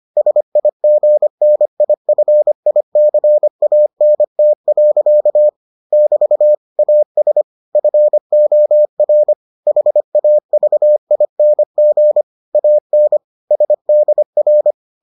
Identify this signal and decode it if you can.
Morse code, heard as a keyed tone, about 25 wpm, 600 Hz. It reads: SIGNIFICANT. <BT> AS FOR HAVING AN SDR